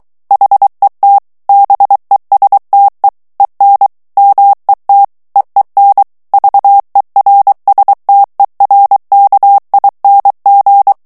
Morse code, heard as a keyed tone, about 23 wpm, 800 Hz. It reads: HET BESTE EN MET EEN VERSTERKING
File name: HOLLANDAIS texte clair 1 vitesse 22 mots minute